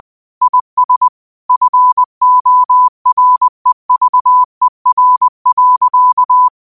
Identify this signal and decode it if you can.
Morse code, heard as a keyed tone, about 20 wpm, 1000 Hz. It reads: IS FOREVER.
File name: qotdfiles / 20wpm